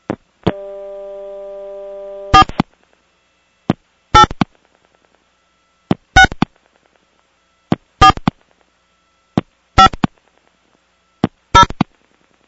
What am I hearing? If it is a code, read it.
DTMF, heard as telephone touch tones, tones of about 90 ms. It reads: 4 8 6 4 5 0
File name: 486 mw